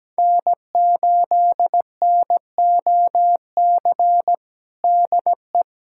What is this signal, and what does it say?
Morse code, heard as a keyed tone, about 17 wpm, 700 Hz. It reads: N8NOC DE
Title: USA Callsign DE Callsign - 5 Characters - 17wpm